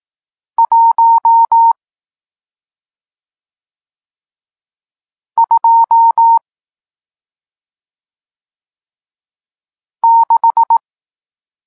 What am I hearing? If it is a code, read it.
Morse code, heard as a keyed tone, about 18 wpm, 920 Hz. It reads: 1 2 6
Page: Tekst 40 – Nauka znaków przestankowych